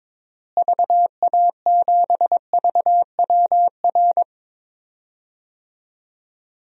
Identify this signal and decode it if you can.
Morse code, heard as a keyed tone, about 22 wpm, 700 Hz. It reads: VA7VWR
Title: DXCC-100 Call Signs - 6 Characters - Repeated 2x - 22wpm